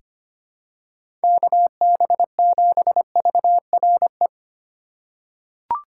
Morse code, heard as a keyed tone, about 25 wpm, 700 Hz. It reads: KB7VRE
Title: USA Call Signs - 6 Characters - 8x Character Spacing - 25wpm